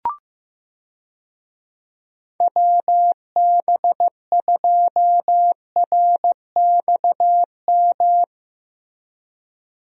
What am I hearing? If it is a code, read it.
Morse code, heard as a keyed tone, about 15 wpm, 700 Hz. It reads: WB2RXM
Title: USA Call Signs - 6 Characters - Repeated 2x - 15wpm